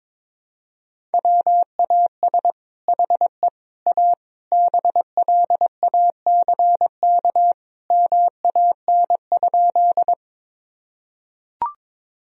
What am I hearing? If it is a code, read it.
Morse code, heard as a keyed tone, about 22 wpm, 700 Hz. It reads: WAS HE A BLACK MAN?
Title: Sentences from Top 1000 Words - Review Part 2 - 22wpm